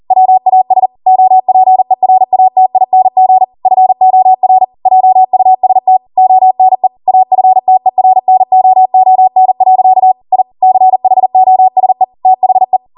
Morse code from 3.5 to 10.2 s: FOR JUST ONE AFTERNOON.